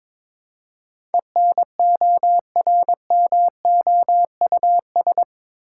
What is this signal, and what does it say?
Morse code, heard as a keyed tone, about 22 wpm, 700 Hz. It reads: ENORMOUS
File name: practice / 22wpm / Eight Letter Words - 22wpm